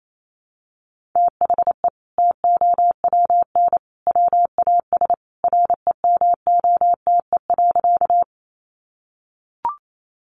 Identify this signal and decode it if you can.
Morse code, heard as a keyed tone, about 28 wpm, 700 Hz. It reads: THE TOWN WAS REMOTE.
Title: Sentences from Top 1300 Words 28wpm